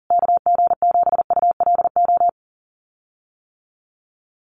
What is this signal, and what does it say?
Morse code, heard as a keyed tone, about 40 wpm, 700 Hz. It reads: KG7ULO
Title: USA Call Signs Repeated 3x 40wpm